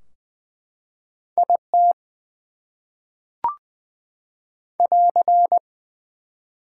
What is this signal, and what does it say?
Morse code, heard as a keyed tone, about 20 wpm, 700 Hz. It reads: IT <AR>
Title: Top 180 QSO Elements 20wpm